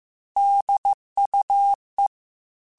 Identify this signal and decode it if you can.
Morse code, heard as a keyed tone, about 15 wpm, 800 Hz. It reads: DUE
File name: Technical Subjects / Technical Subjects-mp3s / AND Gate @13